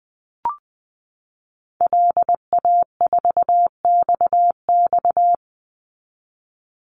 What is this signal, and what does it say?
Morse code, heard as a keyed tone, about 20 wpm, 700 Hz. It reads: LA4XX
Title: Call Signs Repeated 3x - 20wpm